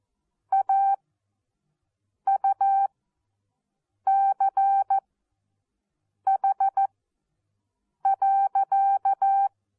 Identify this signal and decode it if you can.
Morse code, heard as a keyed tone, about 14 wpm, spaced as for 6 wpm, 780 Hz. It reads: AUCH.
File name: Onlinelehrgang / morsen / DJ4UF Morsen14